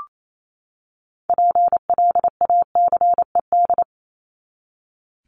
Morse code, heard as a keyed tone, about 28 wpm, 700 Hz. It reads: PLACED
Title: ED Suffix - Top 100 Words with Base Words 28wpm